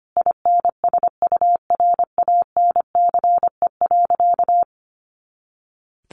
Morse code, heard as a keyed tone, about 25 wpm, 700 Hz. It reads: INSURANCE.
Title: A Sound Of Thunder 25wpm